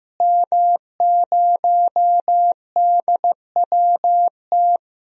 Morse code, heard as a keyed tone, about 15 wpm, 700 Hz. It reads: M0DWT
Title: DXCC-100 Call Signs - 5 Characters - Repeated 3x - 15wpm